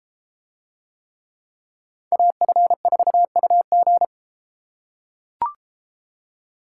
Morse code, heard as a keyed tone, about 33 wpm, 700 Hz. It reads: AF4UG